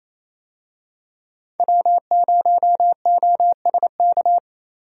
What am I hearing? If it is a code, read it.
Morse code, heard as a keyed tone, about 27 wpm, 700 Hz. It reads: W0OSK
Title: USA Call Signs - 5 Characters - 28wpm